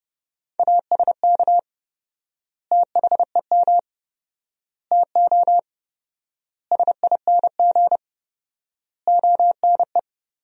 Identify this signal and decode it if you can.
Morse code, heard as a keyed tone, about 32 wpm, 700 Hz. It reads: ASK THEM TO SING ONE
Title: 4x Word-Spacing - Sentences from Top 400 Words - 30wpm